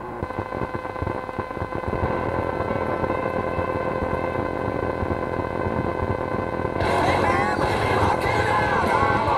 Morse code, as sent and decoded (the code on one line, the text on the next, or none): none